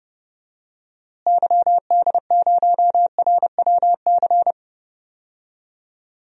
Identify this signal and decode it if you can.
Morse code, heard as a keyed tone, about 32 wpm, 700 Hz. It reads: YD0RWC
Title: DXCC-100 Call Signs - 6 Characters - Repeated 2x - 30wpm